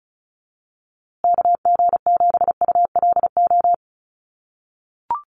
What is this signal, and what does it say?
Morse code, heard as a keyed tone, about 34 wpm, 700 Hz. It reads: KG7ULO
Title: USA Call Signs Repeated 3x 35wpm